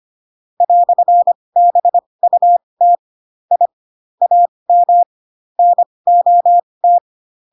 Morse code, heard as a keyed tone, about 25 wpm, spaced as for 20 wpm, 700 Hz. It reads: "BUT I AM NOT
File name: Pinocchio / Pinocchio ch8 15 25 ik1hge